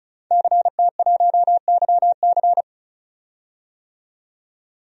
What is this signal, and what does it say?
Morse code, heard as a keyed tone, about 36 wpm, 700 Hz. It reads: CT1YC
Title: DXCC-100 Call Signs - 5 Characters - 35wpm